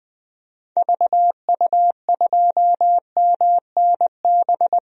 Morse code, heard as a keyed tone, about 20 wpm, 700 Hz. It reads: VU2MNB
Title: DXCC-100 Call Signs - 6 Characters - 20wpm